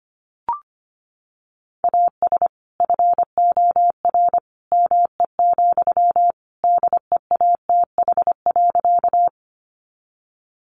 Morse code, heard as a keyed tone, about 25 wpm, 700 Hz. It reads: AS FOR ME, DEATH.
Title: Sentences from Top 1100 Words 25wpm